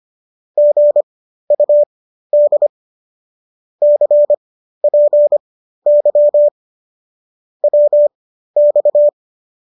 Morse code, heard as a keyed tone, about 25 wpm, spaced as for 13 wpm, 590 Hz. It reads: GUD CPY WX